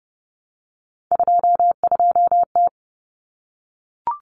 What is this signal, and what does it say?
Morse code, heard as a keyed tone, about 30 wpm, 700 Hz. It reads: 22T
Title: Common Cut Numbers - 3-Digit 30wpm